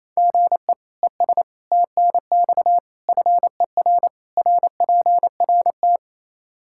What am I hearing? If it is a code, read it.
Morse code, heard as a keyed tone, about 28 wpm, 700 Hz. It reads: GE ES TNX FER RPRT
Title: K1USN SST QSO Practice 28wpm